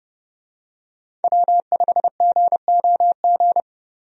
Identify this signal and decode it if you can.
Morse code, heard as a keyed tone, about 32 wpm, 700 Hz. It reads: W5GOG
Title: USA Call Signs - 5 Characters - Repeated 2x - 30wpm